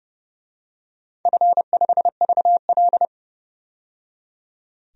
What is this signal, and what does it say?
Morse code, heard as a keyed tone, about 31 wpm, 700 Hz. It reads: F5VL
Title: DXCC-100 Call Signs - 4 Characters - 30wpm